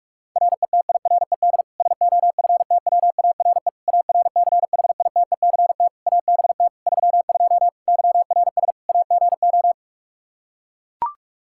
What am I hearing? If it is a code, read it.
Morse code, heard as a keyed tone, about 42 wpm, 700 Hz. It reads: RETIRED SOFTWARE ARCHITECT ABT 32 YRS AGO
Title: Practice QSO - Exchange 3 - More Info - 45wpm